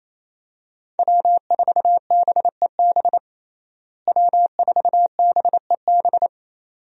Morse code, heard as a keyed tone, about 28 wpm, 700 Hz. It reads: W4BEB W4BEB
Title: USA Call Signs - 5 Characters - Repeated 2x - 28wpm